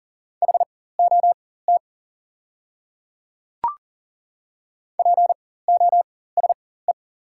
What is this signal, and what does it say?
Morse code, heard as a keyed tone, about 40 wpm, spaced as for 19 wpm, 700 Hz. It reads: HOT POSE
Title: 4x Character Spacing - Top 500 Words - 40wpm